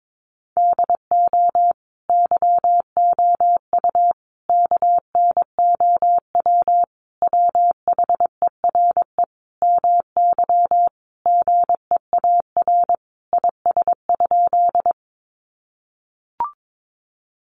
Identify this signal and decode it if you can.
Morse code, heard as a keyed tone, about 22 wpm, 700 Hz. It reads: DO YOU KNOW WHERE MY GEAR IS?
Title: Sentences from Top 1400 Words 22wpm